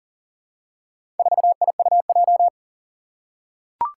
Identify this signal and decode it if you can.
Morse code, heard as a keyed tone, about 38 wpm, 700 Hz. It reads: 4IUJ
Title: Four Challenging Letter-Number Combinations 40wpm